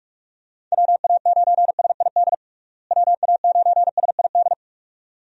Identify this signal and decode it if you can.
Morse code, heard as a keyed tone, about 46 wpm, 700 Hz. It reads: WA9SID WA9SID
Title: USA Call Signs - 6 Characters - Repeated 2x - 45wpm